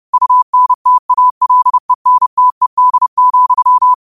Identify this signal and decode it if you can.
Morse code, heard as a keyed tone, about 30 wpm, 1000 Hz. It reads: UNTALENTED,
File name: qotdfiles / 30wpm